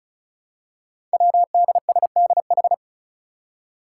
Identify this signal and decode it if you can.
Morse code, heard as a keyed tone, about 36 wpm, 700 Hz. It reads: WDSDH
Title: Five Challenging Letter-Number Combinations 35wpm